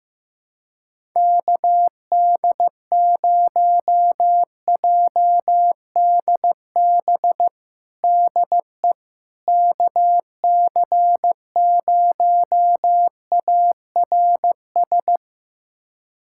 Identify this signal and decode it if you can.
Morse code, heard as a keyed tone, about 15 wpm, 700 Hz. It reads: KD0JDB DE KC0ARS
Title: USA Callsign DE Callsign - 6 Characters - 15wpm